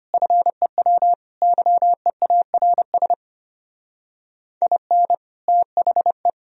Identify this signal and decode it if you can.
Morse code, heard as a keyed tone, about 25 wpm, 700 Hz. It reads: FEW YEARS IN THE
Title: Speed-Racing - Sets of 5 Words 25wpm